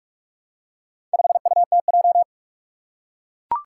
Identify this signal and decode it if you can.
Morse code, heard as a keyed tone, about 44 wpm, 700 Hz. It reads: 5UTJ
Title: Four Challenging Letter-Number Combinations 45wpm